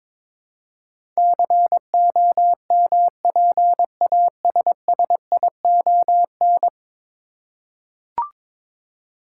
Morse code, heard as a keyed tone, about 22 wpm, 700 Hz. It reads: COMPASSION
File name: practice / 22wpm / 10 Letter Words - First 3 Letters Given - 22wpm